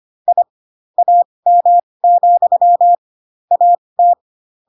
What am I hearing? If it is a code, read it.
Morse code, heard as a keyed tone, about 25 wpm, spaced as for 20 wpm, 700 Hz. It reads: I AM, AT
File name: Pinocchio / Pinocchio ch2 15 25 ik1hge